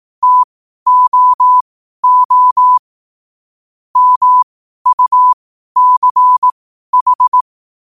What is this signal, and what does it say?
Morse code, heard as a keyed tone, about 18 wpm, spaced as for 12 wpm, 1000 Hz. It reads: TOO MUCH
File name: qotdfiles / 12wpm